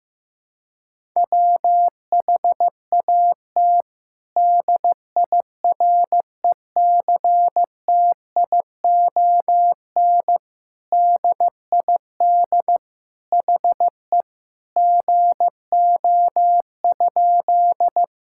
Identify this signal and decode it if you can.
Morse code, heard as a keyed tone, about 15 wpm, 700 Hz. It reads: WHAT DIRECTION DID HE GO?
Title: Sentences from Top 1000 Words - Review Part 5 - 15wpm